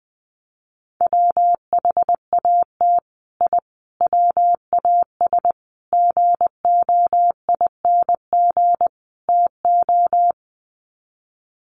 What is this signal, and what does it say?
Morse code, heard as a keyed tone, about 20 wpm, 700 Hz. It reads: WHAT I WAS GOING TO
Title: Sets of 5 Words Encore2 20wpm